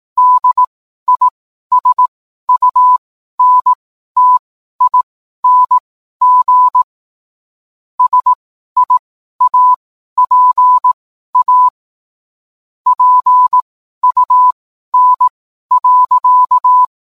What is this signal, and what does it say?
Morse code, heard as a keyed tone, about 18 wpm, spaced as for 12 wpm, 1000 Hz. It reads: DISUNTING SIAPA PUN.